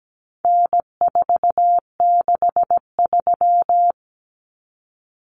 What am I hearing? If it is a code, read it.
Morse code, heard as a keyed tone, about 17 wpm, 700 Hz. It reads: N463
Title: Common Cut Numbers - 4-Digit 17wpm